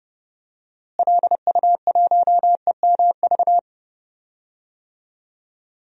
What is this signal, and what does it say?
Morse code, heard as a keyed tone, about 30 wpm, 700 Hz. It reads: LU1EMV
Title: DXCC-100 Call Signs - 6 Characters - Repeated 2x - 30wpm